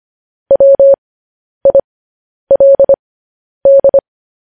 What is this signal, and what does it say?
Morse code, heard as a keyed tone, about 25 wpm, spaced as for 10 wpm, 550 Hz. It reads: WILD
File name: podcast / News headlines-1x-25x10